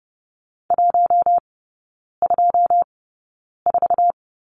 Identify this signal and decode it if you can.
Morse code, heard as a keyed tone, about 30 wpm, spaced as for 9 wpm, 700 Hz. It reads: 124
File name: Lesson 16e - Letter c - Three Characters 30wpm